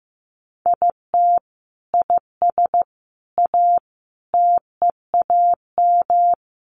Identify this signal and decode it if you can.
Morse code, heard as a keyed tone, about 15 wpm, 700 Hz. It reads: IT IS A TEAM
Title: Sentences from Top 1500 Words 15wpm